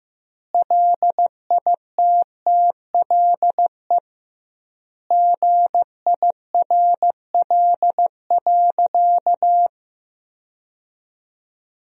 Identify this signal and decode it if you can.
Morse code, heard as a keyed tone, about 16 wpm, 700 Hz. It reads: LITTLE GIRL.
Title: 2x Word-Spacing - Sentences from Top 400 Words - 15wpm